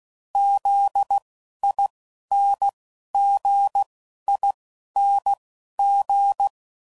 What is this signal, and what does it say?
Morse code, heard as a keyed tone, about 16 wpm, spaced as for 11 wpm, 800 Hz. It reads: ZINGING
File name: Suffixes / Special INGs / Dble INGs